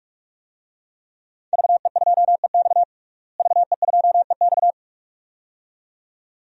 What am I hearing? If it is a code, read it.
Morse code, heard as a keyed tone, about 47 wpm, 700 Hz. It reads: VE2EX VE2EX
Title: DXCC-100 Call Signs - 5 Characters - Repeated 2x - 45wpm